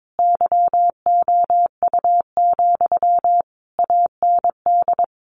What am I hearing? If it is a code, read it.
Morse code, heard as a keyed tone, about 22 wpm, 700 Hz. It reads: YOU, AND
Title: Sentences from Top 1200 Words 22wpm